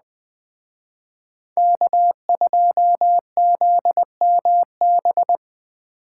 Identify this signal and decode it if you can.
Morse code, heard as a keyed tone, about 20 wpm, 700 Hz. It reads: K2ZMB
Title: Speed Racing - USA 1x3 Call Signs 17wpm